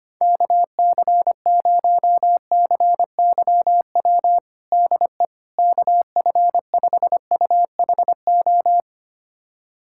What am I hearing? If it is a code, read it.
Morse code, heard as a keyed tone, about 25 wpm, 700 Hz. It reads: KC0CYW DE KF5UHO
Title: USA Callsign DE Callsign - 6 Characters - 25wpm